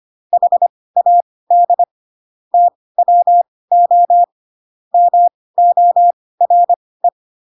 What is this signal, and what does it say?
Morse code, heard as a keyed tone, about 25 wpm, spaced as for 18 wpm, 700 Hz. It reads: HAD TWO MORE
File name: Pinocchio / Pinocchio ch2 12 25 ik1hge